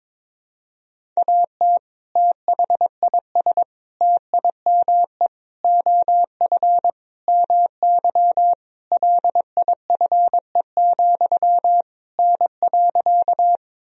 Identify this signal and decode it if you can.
Morse code, heard as a keyed tone, about 22 wpm, 700 Hz. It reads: AT THIS TIME OF MY LIFE, N.
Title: Sentences from Top 300 Words and Abbreviations 22wpm